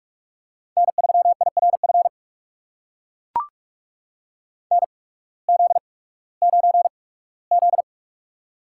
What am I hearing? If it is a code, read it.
Morse code, heard as a keyed tone, about 44 wpm, spaced as for 21 wpm, 700 Hz. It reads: N3IRF NZ9Z
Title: USA Call Signs - 8x Character Spacing - 45wpm